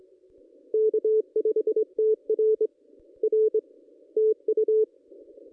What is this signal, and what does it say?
Morse code, heard as a keyed tone, about 25 wpm, 440 Hz. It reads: K5TR R TU
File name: contests / ss-cw-k5tr-op-n5rz -1900